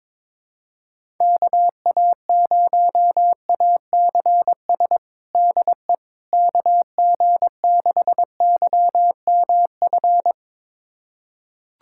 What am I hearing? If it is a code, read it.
Morse code, heard as a keyed tone, about 22 wpm, 700 Hz. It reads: KA0ACS DE KG6YMF
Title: USA Callsign DE Callsign 22wpm